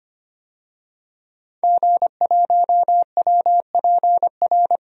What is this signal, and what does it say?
Morse code, heard as a keyed tone, about 25 wpm, 700 Hz. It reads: G1WPR
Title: DXCC-100 Call Signs - 5 Characters - 25wpm